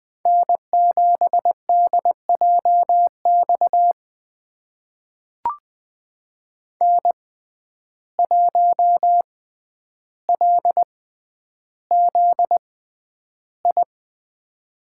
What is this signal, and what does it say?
Morse code, heard as a keyed tone, about 20 wpm, spaced as for 10 wpm, 700 Hz. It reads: N7DJX N1LZI
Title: USA Call Signs - 5 Characters - 6x Character Spacing - 20wpm